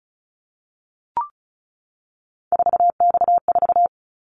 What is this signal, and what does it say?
Morse code, heard as a keyed tone, about 36 wpm, 700 Hz. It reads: 4X4
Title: Callsign Prefixes for Top 50 DXCC - 35wpm